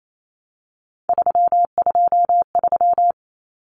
Morse code, heard as a keyed tone, about 28 wpm, 700 Hz. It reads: 323
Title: Common Cut Numbers - 3-Digit 28wpm